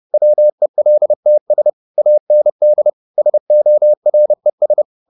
Morse code, heard as a keyed tone, about 30 wpm, 600 Hz. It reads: WELTS AND SORES
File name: the old man and the sea / the old man and the sea 1 30wpm